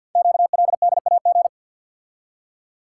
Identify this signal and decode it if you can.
Morse code, heard as a keyed tone, about 48 wpm, 700 Hz. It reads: QLDAG
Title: Code Groups - 5 Letters - 50wpm